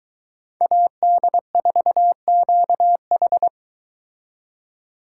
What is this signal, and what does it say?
Morse code, heard as a keyed tone, about 23 wpm, 700 Hz. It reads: AD4QH